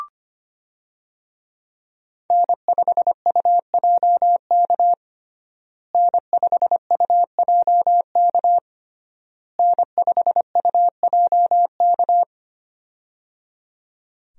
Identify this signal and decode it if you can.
Morse code, heard as a keyed tone, about 25 wpm, 700 Hz. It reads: N5UJK N5UJK N5UJK
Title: USA Call Signs - 5 Characters - Repeated 3x - 25wpm